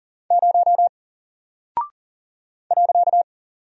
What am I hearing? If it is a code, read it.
Morse code, heard as a keyed tone, about 40 wpm, 700 Hz. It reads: 0 .
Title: Single Character (Letter:Number:Punctuation:Prosign) Alt 40wpm